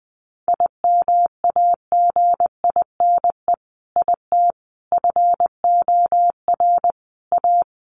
Morse code, heard as a keyed tone, about 19 wpm, 700 Hz. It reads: IMAGINE IT FOR A
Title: Sentences from Top 1400 Words 20wpm